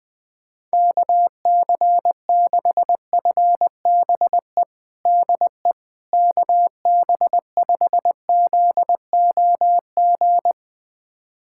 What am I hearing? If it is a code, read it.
Morse code, heard as a keyed tone, about 20 wpm, 700 Hz. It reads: KC6FBE DE KB5ZOG
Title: USA Callsign DE Callsign - 6 Characters - 20wpm